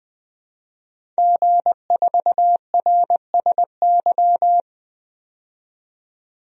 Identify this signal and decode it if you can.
Morse code, heard as a keyed tone, about 21 wpm, 700 Hz. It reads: G4RSY
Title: DXCC-100 Call Signs - 5 Characters - 20wpm